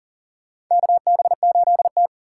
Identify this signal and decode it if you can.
Morse code, heard as a keyed tone, about 37 wpm, 700 Hz. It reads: KB8T
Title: USA Call Signs - 4 Characters - 4x Character Spacing - 40wpm